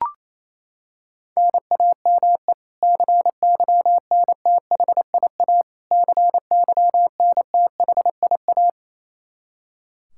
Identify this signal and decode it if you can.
Morse code, heard as a keyed tone, about 28 wpm, 700 Hz. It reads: NAME CYNTHIA CYNTHIA
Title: K1USN SST QSO Practice 28wpm